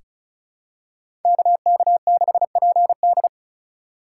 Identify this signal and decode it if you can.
Morse code, heard as a keyed tone, about 36 wpm, 700 Hz. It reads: KK6PD
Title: USA Call Signs - 5 Characters - 3x Character Spacing - 35wpm